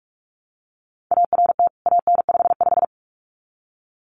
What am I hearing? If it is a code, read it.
Morse code, heard as a keyed tone, about 44 wpm, 700 Hz. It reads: ART AN55
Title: CWT Exchange 45wpm